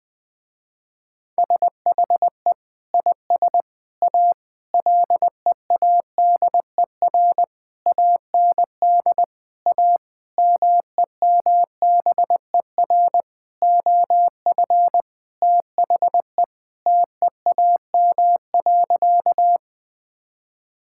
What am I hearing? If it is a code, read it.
Morse code, heard as a keyed tone, about 20 wpm, 700 Hz. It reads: SHE IS A LEADER AND A MEMBER OF THE TEAM.
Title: Sentences with 5-Word Phrases - Spoken Beginning - 20wpm